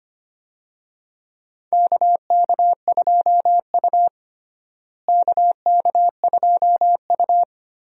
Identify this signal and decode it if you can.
Morse code, heard as a keyed tone, about 25 wpm, 700 Hz. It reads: KK2U KK2U
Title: USA Call Signs - 4 Characters - Repeated 2x - 25wpm